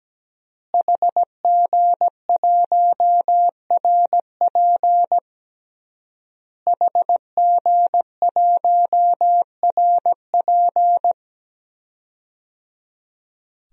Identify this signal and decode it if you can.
Morse code, heard as a keyed tone, about 17 wpm, 700 Hz. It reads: HG1RP HG1RP
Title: DXCC-100 Call Signs - 5 Characters - Repeated 3x - 17wpm